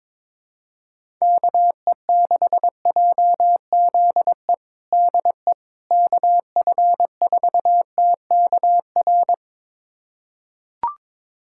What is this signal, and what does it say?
Morse code, heard as a keyed tone, about 22 wpm, 700 Hz. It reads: KE6JZE DE KF4TKR
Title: USA Callsign DE Callsign - 6 Characters - 22wpm